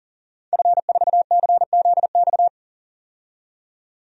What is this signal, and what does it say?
Morse code, heard as a keyed tone, about 41 wpm, 700 Hz. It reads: F4CZX
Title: DXCC-100 Call Signs - 5 Characters - 40wpm